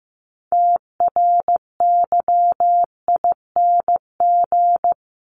Morse code, heard as a keyed tone, about 15 wpm, 700 Hz. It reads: TRYING
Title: ING Suffix - Top 100 Words with Base Words 15wpm